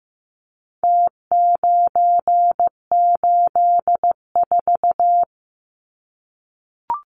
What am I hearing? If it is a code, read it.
Morse code, heard as a keyed tone, about 15 wpm, 700 Hz. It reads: T984
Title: Numbers - Contest Style - 4-Digit 15wpm